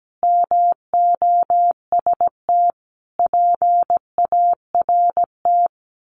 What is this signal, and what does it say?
Morse code, heard as a keyed tone, about 17 wpm, 700 Hz. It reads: MOST PART
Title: Sets of 5 Words Encore 17wpm